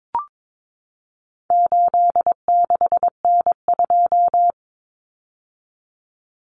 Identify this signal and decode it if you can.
Morse code, heard as a keyed tone, about 22 wpm, 700 Hz. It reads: 86N2
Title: Common Cut Numbers - 4-Digit 22wpm